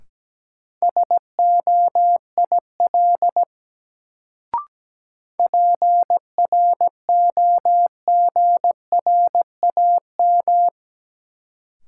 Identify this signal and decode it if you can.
Morse code, heard as a keyed tone, about 17 wpm, 700 Hz. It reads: SOIL PROGRAM
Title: Sentences from Top 1500 Words - New Word Drill Alt - 17wpm